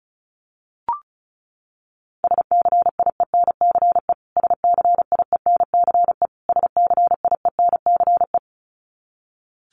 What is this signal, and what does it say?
Morse code, heard as a keyed tone, about 34 wpm, 700 Hz. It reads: SCIENCE SCIENCE SCIENCE
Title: Top 500 Words Repeated 35wpm